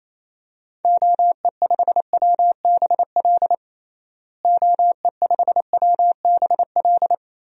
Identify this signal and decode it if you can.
Morse code, heard as a keyed tone, about 27 wpm, 700 Hz. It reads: OE5WBL OE5WBL
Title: DXCC-100 Call Signs - 6 Characters - Repeated 2x - 28wpm